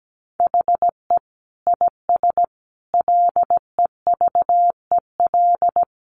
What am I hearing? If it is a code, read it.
Morse code, heard as a keyed tone, about 17 wpm, 700 Hz. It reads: HE IS LEVEL
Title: Sentences from Top 700 Words 17wpm